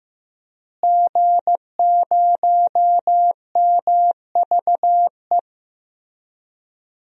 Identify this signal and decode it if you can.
Morse code, heard as a keyed tone, about 15 wpm, 700 Hz. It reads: G0MVE
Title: DXCC-100 Call Signs - 5 Characters - Repeated 3x - 15wpm